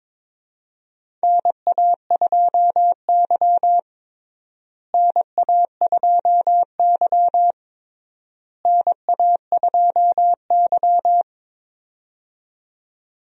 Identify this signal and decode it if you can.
Morse code, heard as a keyed tone, about 22 wpm, 700 Hz. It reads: NA2Y NA2Y NA2Y
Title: USA Call Signs - 4 Characters - Repeated 3x - 22wpm